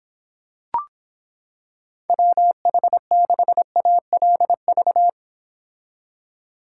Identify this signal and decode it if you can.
Morse code, heard as a keyed tone, about 26 wpm, 700 Hz. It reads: WH6ALV